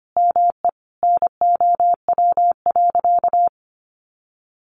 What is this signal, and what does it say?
Morse code, heard as a keyed tone, about 25 wpm, 700 Hz. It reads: ME NOW.
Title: Sentences from Top 1300 Words 25wpm